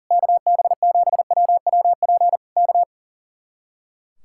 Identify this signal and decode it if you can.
Morse code, heard as a keyed tone, about 37 wpm, 700 Hz. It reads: KB7WWP K
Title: Speed-Racing Common QSO Phrases - 40wpm